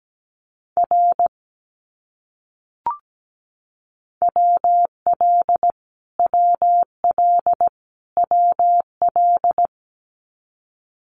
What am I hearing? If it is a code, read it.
Morse code, heard as a keyed tone, about 17 wpm, 700 Hz. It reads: R WL WL WL
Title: Abbreviations Repeated 3x - 17wpm